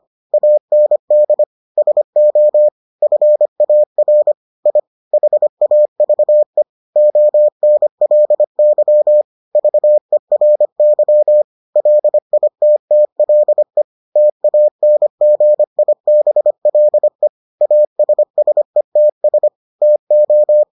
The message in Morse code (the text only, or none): AND SO FAR I HAVE ONLY VERY LITTLE TANGIBLE ASSETS TO